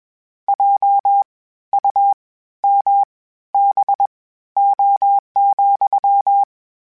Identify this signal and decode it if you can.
Morse code, heard as a keyed tone, about 21 wpm, spaced as for 13 wpm, 800 Hz. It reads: JUMBO,
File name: Misc / Misc-mp3s / WordPairs sp @20